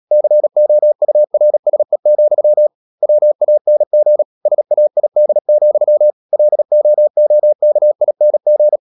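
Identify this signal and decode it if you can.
Morse code, heard as a keyed tone, about 37 wpm, 600 Hz. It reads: COURSE, WANG SAID, LOOKING
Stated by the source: Morse code practice at 35–39 wpm sending English text of Ken Follett's novel Never, from CW med Gnister.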